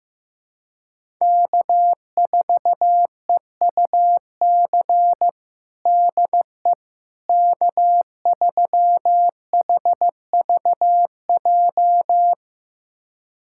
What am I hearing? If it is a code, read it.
Morse code, heard as a keyed tone, about 15 wpm, 700 Hz. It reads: K4EUC DE K3HVJ
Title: USA Callsign DE Callsign - 5 Characters - 15wpm